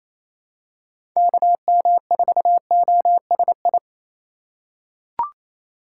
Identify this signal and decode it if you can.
Morse code, heard as a keyed tone, about 28 wpm, 700 Hz. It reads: KM4OSI
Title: USA Call Signs - 6 Characters - 3x Character Spacing - 28wpm